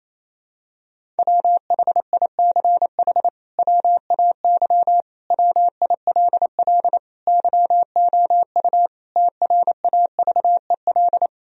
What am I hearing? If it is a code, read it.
Morse code, heard as a keyed tone, about 27 wpm, 700 Hz. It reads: WHICH WAY WILL YOU TRAVEL
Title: Sentences from Top 1000 Words - Review Part 2 - 28wpm